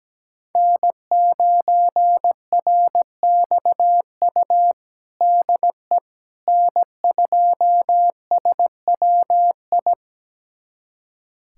Morse code, heard as a keyed tone, about 17 wpm, 700 Hz. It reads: N9RXU DE N2SWI
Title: USA Callsign DE Callsign - 5 Characters - 17wpm